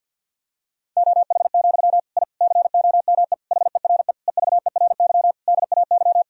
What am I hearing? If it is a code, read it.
Morse code, heard as a keyed tone, about 51 wpm, 700 Hz. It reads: OH, I COME HERE EVERY DAY
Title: Sentences from Top 1000 Words - Review Part 2 - 50wpm